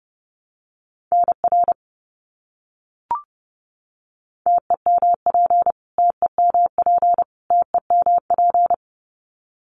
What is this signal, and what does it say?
Morse code, heard as a keyed tone, about 30 wpm, 700 Hz. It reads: NR TEMP TEMP TEMP
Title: Abbreviations Repeated 3x - 30wpm